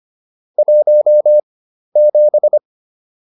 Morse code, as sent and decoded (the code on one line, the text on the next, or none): .---- --...
17